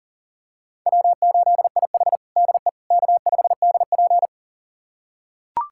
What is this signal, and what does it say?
Morse code, heard as a keyed tone, about 40 wpm, 700 Hz. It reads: W8IH DE K5DP
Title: USA Callsign DE Callsign - 4 Characters - 40wpm